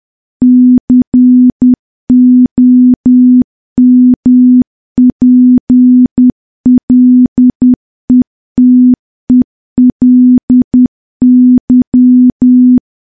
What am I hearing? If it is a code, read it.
Morse code, heard as a keyed tone, about 10 wpm, 250 Hz. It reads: COMPLETELY
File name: morse / morse-training-10WPM